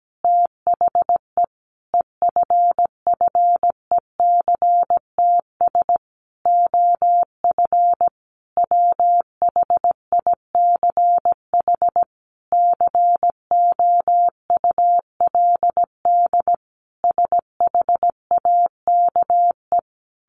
Morse code, heard as a keyed tone, about 17 wpm, 700 Hz. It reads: THE EFFECTS OF WHICH COULD SHAKE
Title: A Sound Of Thunder 17wpm